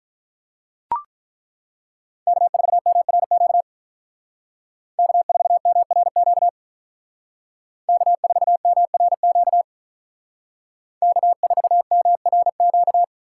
Morse code, heard as a keyed tone, about 42 wpm, 700 Hz. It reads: K4MRQ K4MRQ K4MRQ K4MRQ
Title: Speed Racing - USA 1x3 Call Signs 35wpm